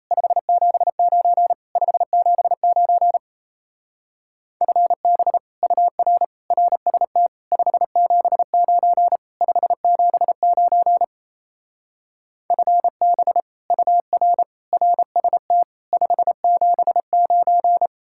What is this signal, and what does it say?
Morse code, heard as a keyed tone, about 35 wpm, 700 Hz. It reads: 579 579 FB UR RST 579 579 FB UR RST 579